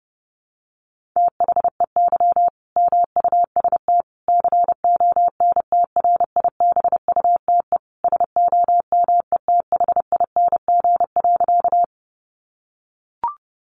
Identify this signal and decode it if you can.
Morse code, heard as a keyed tone, about 30 wpm, 700 Hz. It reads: THEY MUST CONTRIBUTE SOMETHING.
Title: Sentences from Top 1400 Words 30wpm